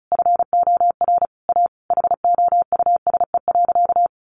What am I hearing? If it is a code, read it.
Morse code, heard as a keyed tone, about 34 wpm, 700 Hz. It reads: FOR A HOUSE.
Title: Sentences from Top 600 Words 35wpm